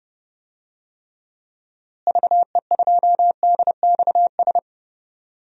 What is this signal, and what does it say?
Morse code, heard as a keyed tone, about 30 wpm, 700 Hz. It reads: VE2DXS
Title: DXCC-100 Call Signs - 6 Characters - Repeated 3x - 30wpm